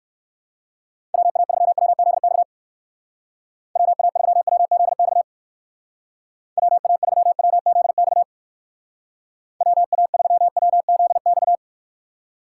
Morse code, heard as a keyed tone, about 56 wpm, 700 Hz. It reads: WA3WZX WA3WZX WA3WZX WA3WZX
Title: Speed Racing - USA 2x3 Call Signs 45wpm